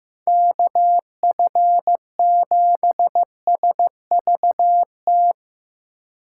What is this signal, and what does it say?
Morse code, heard as a keyed tone, about 17 wpm, 700 Hz. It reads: KF7SVT
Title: Speed Racing - USA 2x3 Call Signs 15wpm